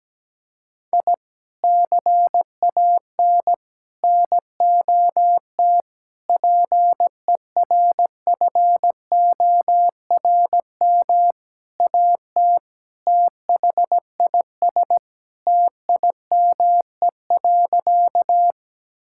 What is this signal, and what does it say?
Morse code, heard as a keyed tone, about 17 wpm, 700 Hz. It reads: I CAN NOT PERFORM AT THIS TIME.
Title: Sentences from Top 1000 Words - Review Part 5 - 17wpm